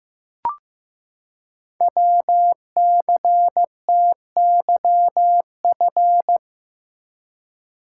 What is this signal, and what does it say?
Morse code, heard as a keyed tone, about 15 wpm, 700 Hz. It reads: WCTYF
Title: Code Groups - 5 Letter:Numbers - 15wpm